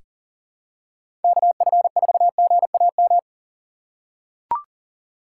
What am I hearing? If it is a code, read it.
Morse code, heard as a keyed tone, about 40 wpm, 700 Hz. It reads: KF4GAM
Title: USA Call Signs - 6 Characters - 3x Character Spacing - 40wpm